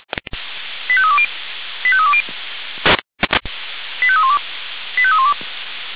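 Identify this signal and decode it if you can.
Five-tone selective call (ZVEI), heard as quick runs of five tones, tones of about 70 ms. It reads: 85320 85320 85312 85312
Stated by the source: ZVEI five-tone call received by SDR.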